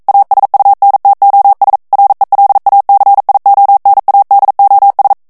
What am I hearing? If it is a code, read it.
Morse code, heard as a keyed tone, about 42 wpm, 800 Hz. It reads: ASUNTOS RELACIONADOS
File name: morse mp3 / ESPAGNOL TEXTE EN CLAIR 1 VITESSE 40 MOTS MINUTES